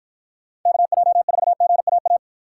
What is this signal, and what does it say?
Morse code, heard as a keyed tone, about 51 wpm, 700 Hz. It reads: KJ4GAA